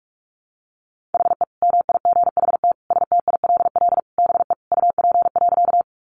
Morse code, heard as a keyed tone, about 44 wpm, 700 Hz. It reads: HE MIGHT STILL BE UP.
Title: Sentences from Top 300 Words 45wpm